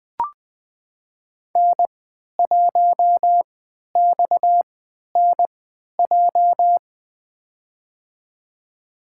Morse code, heard as a keyed tone, about 20 wpm, spaced as for 11 wpm, 700 Hz. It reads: N1XNJ
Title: USA Call Signs - 5 Characters - 3x Character Spacing - 20wpm